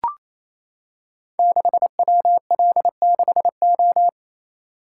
Morse code, heard as a keyed tone, about 28 wpm, 700 Hz. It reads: 6WL6O
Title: Code Groups - 5 Letter:Numbers - 28wpm